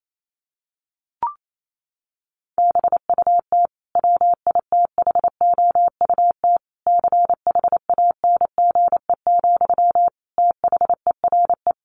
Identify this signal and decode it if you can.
Morse code, heard as a keyed tone, about 28 wpm, 700 Hz. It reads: BUT WITHOUT CHANGE, THERE
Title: Sentences from Top 1400 Words 28wpm